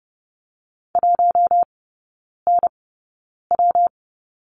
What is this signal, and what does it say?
Morse code, heard as a keyed tone, about 30 wpm, 700 Hz. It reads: 1 N W
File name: Lesson 19e - Letter w - Three Characters 30wpm